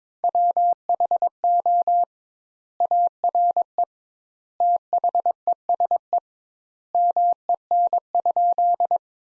Morse code, heard as a keyed tone, about 21 wpm, 700 Hz. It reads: WHO ARE THESE MEN?
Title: Sentences Top 200 Words - 22wpm